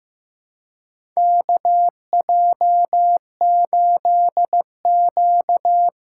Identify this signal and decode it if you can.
Morse code, heard as a keyed tone, about 15 wpm, 700 Hz. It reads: KJ8Q
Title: USA Call Signs - 3x Character Spacing - 15wpm